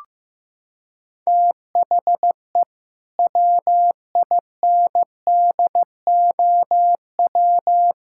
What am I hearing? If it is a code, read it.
Morse code, heard as a keyed tone, about 15 wpm, 700 Hz. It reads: THE WINDOW
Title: Sentences from Top 1600 Words 15wpm